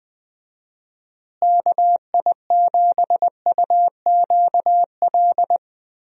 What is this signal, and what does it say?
Morse code, heard as a keyed tone, about 20 wpm, 700 Hz. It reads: KI7UQL
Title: USA Call Signs - 6 Characters - Repeated 2x - 20wpm